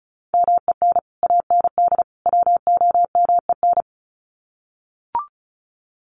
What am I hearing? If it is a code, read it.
Morse code, heard as a keyed tone, about 34 wpm, 700 Hz. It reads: MEN AND WOMEN
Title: Sets of 3 Words 35wpm